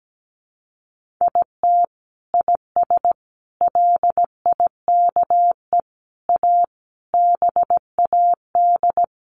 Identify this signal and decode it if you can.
Morse code, heard as a keyed tone, about 17 wpm, 700 Hz. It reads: IT IS LIKE A BAD
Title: Sentences from Top 1500 Words 17wpm